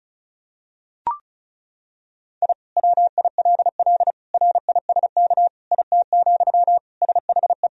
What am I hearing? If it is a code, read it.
Morse code, heard as a keyed tone, about 34 wpm, 700 Hz. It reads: I WILL RISK IT, SHE
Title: Sentences from Top 1000 Words - Review Part 4 - 35wpm